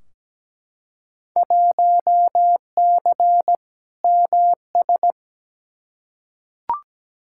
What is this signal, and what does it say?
Morse code, heard as a keyed tone, about 17 wpm, 700 Hz. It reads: 1C MS
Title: Field Day Exchanges Encore - 17wpm